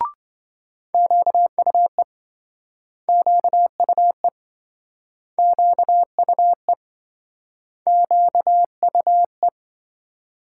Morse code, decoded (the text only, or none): QUE QUE QUE QUE